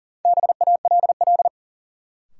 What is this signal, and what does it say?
Morse code, heard as a keyed tone, about 40 wpm, 700 Hz. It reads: BALL
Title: Sentences from Top 900 Words - New Word Drill - 40wpm